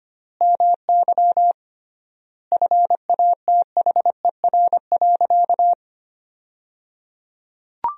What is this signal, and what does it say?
Morse code, heard as a keyed tone, about 25 wpm, 700 Hz. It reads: MY FATHER.
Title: 3x Word-Spacing - Sentences from Top 500 Words - 25wpm